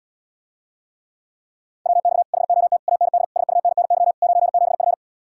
Morse code, heard as a keyed tone, about 80 wpm, 700 Hz. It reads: WX HOT AND SUNNY 97F